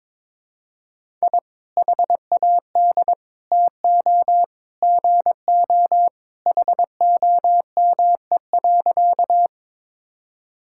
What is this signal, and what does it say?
Morse code, heard as a keyed tone, about 22 wpm, 700 Hz. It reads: I HAD TO GO HOME.
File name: practice / 22wpm / Sentences with 3-Word Phrases - 22wpm